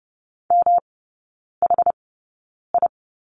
Morse code, heard as a keyed tone, about 30 wpm, spaced as for 9 wpm, 700 Hz. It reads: MHI
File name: Lesson 19e - Letter w - Three Characters 30wpm